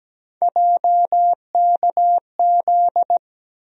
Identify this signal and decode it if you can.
Morse code, heard as a keyed tone, about 17 wpm, 700 Hz. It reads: JKZ
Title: USA Call Sign Postfixes - 3 Letters 17wpm